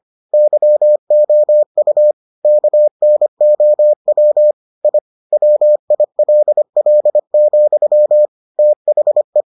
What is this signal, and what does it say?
Morse code, heard as a keyed tone, about 25 wpm, 600 Hz. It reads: YOU KNOW I WILL, THE